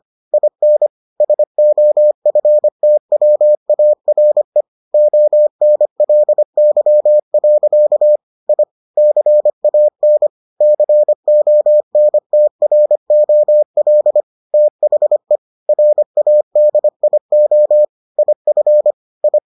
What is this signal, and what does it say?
Morse code, heard as a keyed tone, about 25 wpm, 600 Hz. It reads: IN SOFTWARE ONLY. I CAN CONTROL THE RADIO IF I